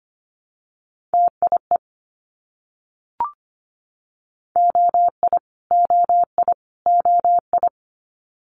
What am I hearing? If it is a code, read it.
Morse code, heard as a keyed tone, about 25 wpm, 700 Hz. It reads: TIE OI OI OI
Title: Multi-letter Phonemes Repeated 3x 25wpm